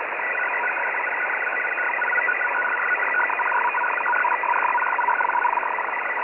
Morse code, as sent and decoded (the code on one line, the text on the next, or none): none